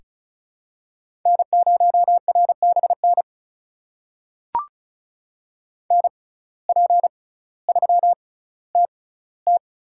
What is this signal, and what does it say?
Morse code, heard as a keyed tone, about 36 wpm, spaced as for 18 wpm, 700 Hz. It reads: N0RBN NP3TT
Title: USA Call Signs - 5 Characters - 6x Character Spacing - 35wpm